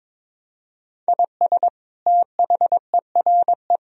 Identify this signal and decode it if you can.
Morse code, heard as a keyed tone, about 21 wpm, 700 Hz. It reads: IS THERE
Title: Sentences from Top 1000 Words - Review Part 4 - 22wpm